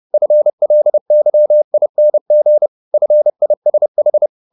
Morse code, heard as a keyed tone, about 30 wpm, 600 Hz. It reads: FLYING FISH
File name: the old man and the sea / the old man and the sea 1 30wpm